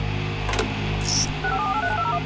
DTMF, heard as telephone touch tones, tones of about 80 ms. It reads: 574839201
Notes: Mit einer Spielzeugpfeife, die er in seiner Müslipackung ( "Cap'n Crunch") fand, manipulierte er in den 70er-Jahren das weltweite Telefonnetz.